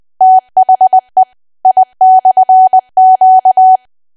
Morse code, heard as a keyed tone, about 20 wpm, 750 Hz. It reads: THE I/Q
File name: cwa / W1AW-Level3 / 20WPM